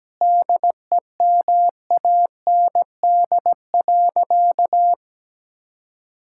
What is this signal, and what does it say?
Morse code, heard as a keyed tone, about 17 wpm, 700 Hz. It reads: DEMAND.
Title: Sentences from Top 1000 Words - Review Part 5 - 17wpm